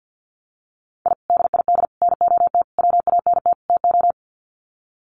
Morse code, heard as a keyed tone, about 50 wpm, 700 Hz. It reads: I DID NOT WANT TO
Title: Sets of 5 Words 50wpm